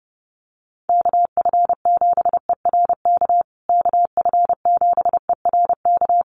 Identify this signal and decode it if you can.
Morse code, heard as a keyed tone, about 30 wpm, 700 Hz. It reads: KF7ERK KF7ERK
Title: USA Call Signs Repeated 3x 30wpm